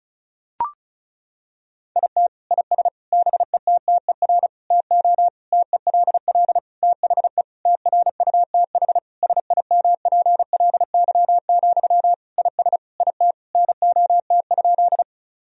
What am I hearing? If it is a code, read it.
Morse code, heard as a keyed tone, about 34 wpm, 700 Hz. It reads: IT IS BETTER TO TELL THE TRUTH SIMPLY, IS IT NOT?
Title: Sentences from Top 1000 Words - Review Part 4 - 35wpm